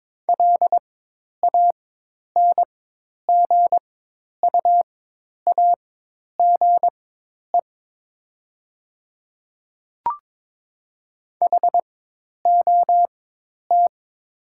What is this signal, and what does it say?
Morse code, heard as a keyed tone, about 22 wpm, spaced as for 10 wpm, 700 Hz. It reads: LANGUAGE HOT
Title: 4x Character Spacing - Top 500 Words - 22wpm